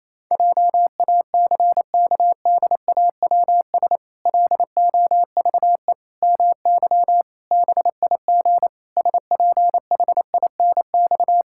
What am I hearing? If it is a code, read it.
Morse code, heard as a keyed tone, about 28 wpm, 700 Hz. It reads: JACKDAWS LOVE MY BIG SPHINX
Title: Pangram Challenge v2 - 28wpm